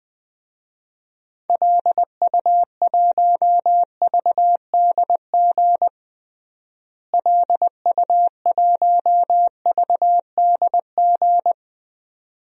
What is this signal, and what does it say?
Morse code, heard as a keyed tone, about 20 wpm, 700 Hz. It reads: LU1VDG LU1VDG
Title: DXCC-100 Call Signs - 6 Characters - Repeated 2x - 20wpm